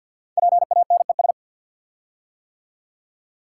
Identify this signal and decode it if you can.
Morse code, heard as a keyed tone, about 50 wpm, 700 Hz. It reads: PANES
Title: Code Groups - 5 Letters - 50wpm